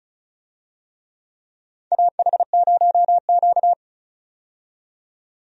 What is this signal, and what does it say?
Morse code, heard as a keyed tone, about 34 wpm, 700 Hz. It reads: AH0Q
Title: USA Call Signs - 4 Characters - 35wpm